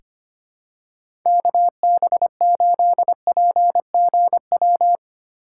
Morse code, heard as a keyed tone, about 25 wpm, 700 Hz. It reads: KB8PGW
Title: K1USN SST QSO Practice - 25wpm